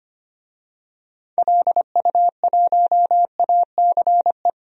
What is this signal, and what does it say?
Morse code, heard as a keyed tone, about 25 wpm, 700 Hz. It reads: LU1ACE
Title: DXCC-100 Call Signs - 6 Characters - 25wpm